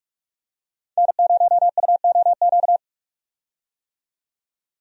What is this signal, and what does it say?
Morse code, heard as a keyed tone, about 46 wpm, 700 Hz. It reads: N0UOQ
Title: USA Call Signs - 5 Characters - 45wpm